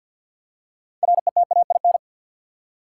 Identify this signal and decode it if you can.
Morse code, heard as a keyed tone, about 52 wpm, 700 Hz. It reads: RETAIN
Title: Six Letter Words - 50wpm